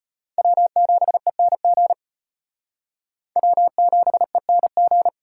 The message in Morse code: W7ENG W7ENG

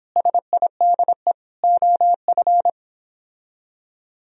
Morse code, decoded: SIDE OF

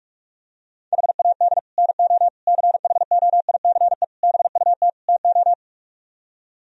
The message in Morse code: HAD NO CHOICE BUT TO

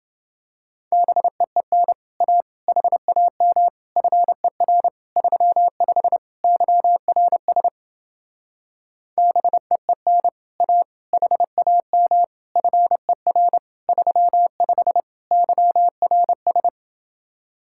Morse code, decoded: BEEN A HAM FER 35 YRS BEEN A HAM FER 35 YRS

